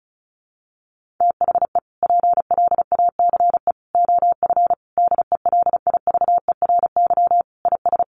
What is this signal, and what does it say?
Morse code, heard as a keyed tone, about 34 wpm, 700 Hz. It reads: THE PLACE OF DELIVERY IS
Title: Sentences from Top 1500 Words 35wpm